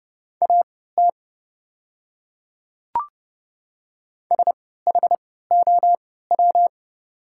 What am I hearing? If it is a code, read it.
Morse code, heard as a keyed tone, about 29 wpm, spaced as for 17 wpm, 700 Hz. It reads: AT SHOW
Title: 3x Character Spacing - Top 500 Words - 30wpm